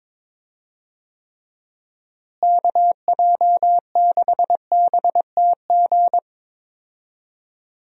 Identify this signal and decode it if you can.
Morse code, heard as a keyed tone, about 22 wpm, 700 Hz. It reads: KJ6BTG
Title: USA Call Signs - 6 Characters - Repeated 3x - 22wpm